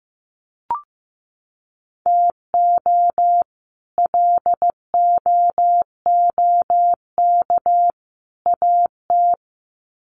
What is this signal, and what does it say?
Morse code, heard as a keyed tone, about 15 wpm, 700 Hz. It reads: TO LOOK AT
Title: Sets of 3 Words 15wpm